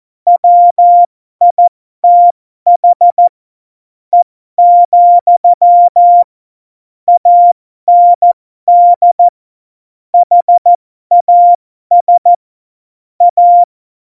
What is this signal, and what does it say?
Morse code, heard as a keyed tone, about 14 wpm, spaced as for 12 wpm, 700 Hz. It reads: WITH E, AND HAS A
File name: Morsekurs / Witze